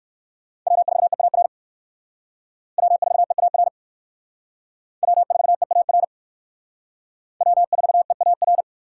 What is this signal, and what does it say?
Morse code, heard as a keyed tone, about 61 wpm, 700 Hz. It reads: W4EAR W4EAR W4EAR W4EAR